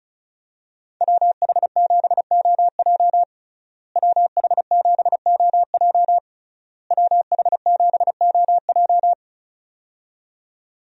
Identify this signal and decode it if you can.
Morse code, heard as a keyed tone, about 36 wpm, 700 Hz. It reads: WH7OJ WH7OJ WH7OJ
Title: USA Call Signs - 5 Characters - Repeated 3x - 35wpm